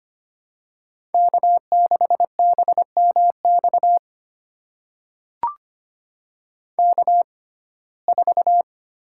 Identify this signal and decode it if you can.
Morse code, heard as a keyed tone, about 25 wpm, spaced as for 18 wpm, 700 Hz. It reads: K6BMX K4
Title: USA Call Signs - 6x Character Spacing - 25wpm